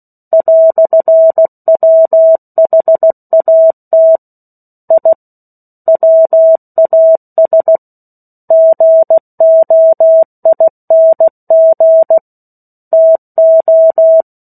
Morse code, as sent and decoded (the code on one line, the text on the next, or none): .-..-. .-- .... .- - / .. / .-- .- ... / --. --- .. -. --. / - ---
"WHAT I WAS GOING TO